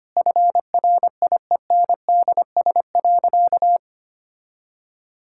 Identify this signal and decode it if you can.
Morse code, heard as a keyed tone, about 25 wpm, 700 Hz. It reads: FRIENDS.
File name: practice / 25wpm / Difficult Sentences with Hints - 25wpm